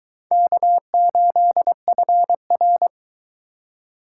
Morse code, heard as a keyed tone, about 22 wpm, 700 Hz. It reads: K8FR